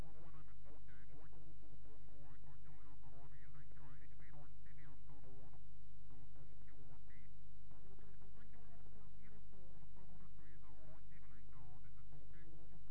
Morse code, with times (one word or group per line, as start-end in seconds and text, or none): none